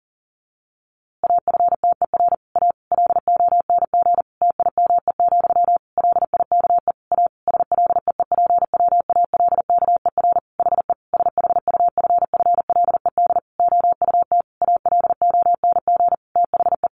Morse code, heard as a keyed tone, about 40 wpm, 700 Hz. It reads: AFTER A LONG TIME, LIKE A SLEEPWALKER HE SHUFFLED OUT ALONG THE
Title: A Sound Of Thunder 40wpm